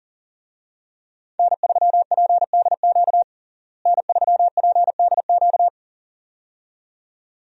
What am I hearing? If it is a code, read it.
Morse code, heard as a keyed tone, about 40 wpm, 700 Hz. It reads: N3PDQ N3PDQ
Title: USA Call Signs - 5 Characters - Repeated 2x - 40wpm